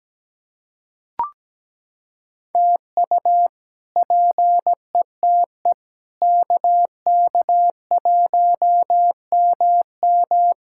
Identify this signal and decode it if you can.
Morse code, heard as a keyed tone, about 17 wpm, 700 Hz. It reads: TU PETE KK1MM
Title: K1USN SST QSO Practice - 17wpm